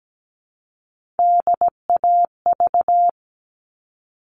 Morse code, heard as a keyed tone, about 17 wpm, 700 Hz. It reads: DAV
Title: All Cut Numbers - 3-Digit 17wpm